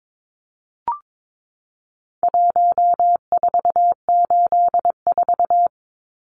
Morse code, heard as a keyed tone, about 22 wpm, 700 Hz. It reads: 1484
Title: Numbers - Contest Style - 4-Digit 22wpm